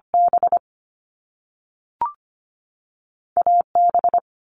Morse code, heard as a keyed tone, about 25 wpm, 700 Hz. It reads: B A B
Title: Top Two Letter Combinations in Words 25wpm